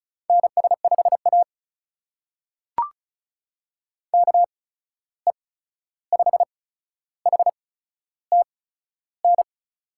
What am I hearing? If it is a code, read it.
Morse code, heard as a keyed tone, about 35 wpm, spaced as for 13 wpm, 700 Hz. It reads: NS5A KE5HTN
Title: USA Call Signs - 8x Character Spacing - 35wpm